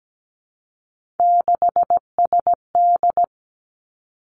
Morse code, heard as a keyed tone, about 17 wpm, 700 Hz. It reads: 6SD
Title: Three Challenging Letter-Number Combinations 17wpm